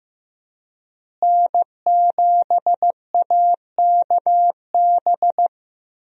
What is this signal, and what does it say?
Morse code, heard as a keyed tone, about 15 wpm, 700 Hz. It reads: N7AKB
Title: USA Call Signs - 5 Characters - Repeated 2x - 15wpm